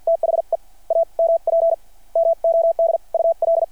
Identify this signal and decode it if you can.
Morse code, heard as a keyed tone, about 50 wpm, spaced as for 41 wpm, 660 Hz. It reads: THE AMP MODUL
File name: archived / CRSnet